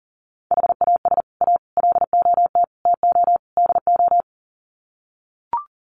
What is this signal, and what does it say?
Morse code, heard as a keyed tone, about 40 wpm, 700 Hz. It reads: HAS A LOT TO DO
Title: Sets of 5 Words Encore2 40wpm